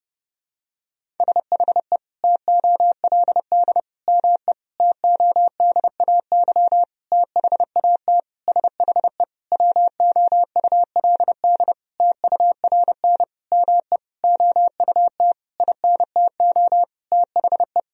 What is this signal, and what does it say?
Morse code, heard as a keyed tone, about 30 wpm, 700 Hz. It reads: SHE TOLD ME TODAY THAT SHE WOULD TURN ME OUT INTO THE
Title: Sentences from Top 1000 Words - Review Part 4 - 30wpm